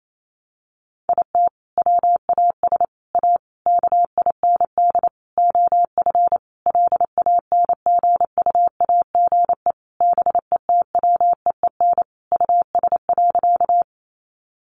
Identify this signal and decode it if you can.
Morse code, heard as a keyed tone, about 28 wpm, 700 Hz. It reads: IT WAS A KIND OF LANGUAGE BETWEEN US.
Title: Sentences from Top 500 Words 28wpm